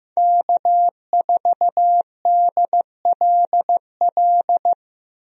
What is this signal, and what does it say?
Morse code, heard as a keyed tone, about 15 wpm, 700 Hz. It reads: K4DLL
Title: USA Call Signs - 5 Characters - Repeated 2x - 15wpm